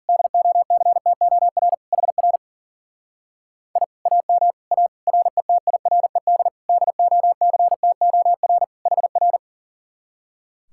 Morse code, heard as a keyed tone, about 49 wpm, 700 Hz. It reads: DOCTOR HR I AM A RETIRED DOCTOR HR
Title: Speed-Racing Common QSO Phrases - 40wpm